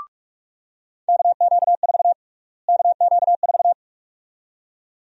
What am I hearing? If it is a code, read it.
Morse code, heard as a keyed tone, about 44 wpm, 690 Hz. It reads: KQ4 KQ4
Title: USA Call Sign Prefixes - Repeated 2x - 45wpm